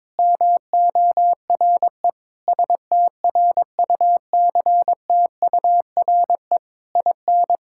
Morse code, heard as a keyed tone, about 22 wpm, 700 Hz. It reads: MORE STRUCTURE IN
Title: Sentences from Top 1000 Words - Review Part 6 - 22wpm